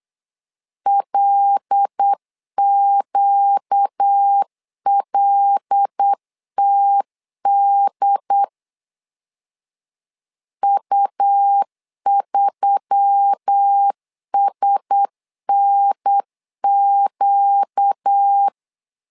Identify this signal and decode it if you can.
Morse code, heard as a keyed tone, about 8 wpm, 800 Hz. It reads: LQLTD U3SNQ
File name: LEZIONI / 5 LBVC / 5 L 8wpm 2831